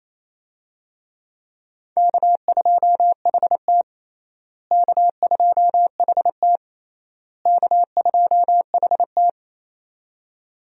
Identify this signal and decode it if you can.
Morse code, heard as a keyed tone, about 28 wpm, 700 Hz. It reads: K2HT K2HT K2HT
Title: USA Call Signs - 4 Characters - Repeated 3x - 28wpm